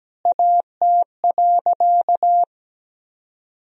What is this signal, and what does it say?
Morse code, heard as a keyed tone, about 17 wpm, 700 Hz. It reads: AT.
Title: 2x Word-Spacing - Sentences from Top 200 Words - 17wpm